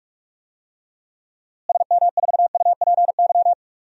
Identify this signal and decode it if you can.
Morse code, heard as a keyed tone, about 43 wpm, 700 Hz. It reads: SM4UPY
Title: DXCC-100 Call Signs - 6 Characters - Repeated 2x - 45wpm